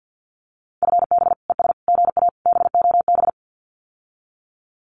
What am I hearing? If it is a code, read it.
Morse code, heard as a keyed tone, about 50 wpm, 700 Hz. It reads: FB ES GA BOB
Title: Common QSO Phrases Encore 50wpm